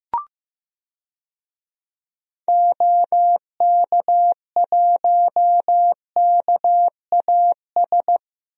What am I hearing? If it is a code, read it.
Morse code, heard as a keyed tone, about 15 wpm, 700 Hz. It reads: OK1KAS
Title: DXCC-100 Call Signs - 6 Characters - Repeated 3x - 15wpm